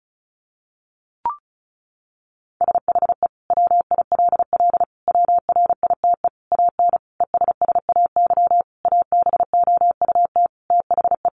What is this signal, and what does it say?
Morse code, heard as a keyed tone, about 34 wpm, 700 Hz. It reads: SHE WILL WRITE AN ESSAY ABOUT THE
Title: Sentences from Top 1500 Words 35wpm